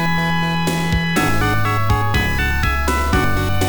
There is no code.